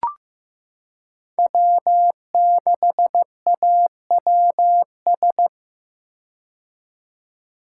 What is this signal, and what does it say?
Morse code, heard as a keyed tone, about 15 wpm, 700 Hz. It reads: W6AWS
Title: K1USN SST QSO Practice - 15wpm